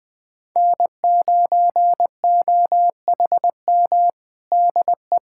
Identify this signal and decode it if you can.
Morse code, heard as a keyed tone, about 21 wpm, 700 Hz. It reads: N9OHM DE
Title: USA Callsign DE Callsign 20wpm